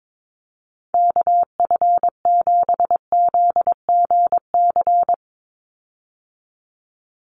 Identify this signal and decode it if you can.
Morse code, heard as a keyed tone, about 22 wpm, 700 Hz. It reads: KF7ZGC
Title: USA Call Signs 22wpm